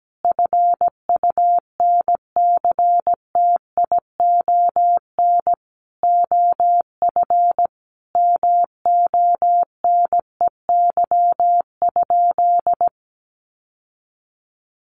Morse code, heard as a keyed tone, about 17 wpm, 700 Hz. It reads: FUNCTION OF MONEY?
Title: Sentences from Top 1100 Words 17wpm